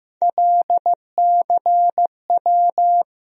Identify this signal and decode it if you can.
Morse code, heard as a keyed tone, about 15 wpm, 700 Hz. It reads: LCW
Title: USA Call Sign Postfixes - 3 Letters 15wpm